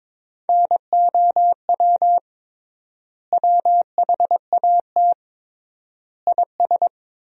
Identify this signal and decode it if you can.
Morse code, heard as a keyed tone, about 22 wpm, 700 Hz. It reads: NOW WHAT IS
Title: 3x Word-Spacing - Sentences from Top 200 Words - 22wpm